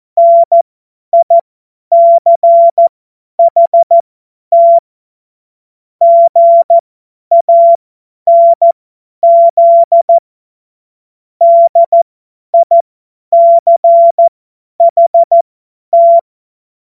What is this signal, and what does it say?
Morse code, heard as a keyed tone, about 14 wpm, spaced as for 10 wpm, 680 Hz. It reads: NICHT GANZ DICHT
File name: Morsekurs / Witze / Witz 16. November 2021, 50 F70